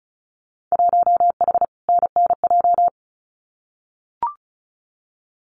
Morse code, heard as a keyed tone, about 36 wpm, 690 Hz. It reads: 1H NNJ
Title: Winter Field Day Exchanges 35wpm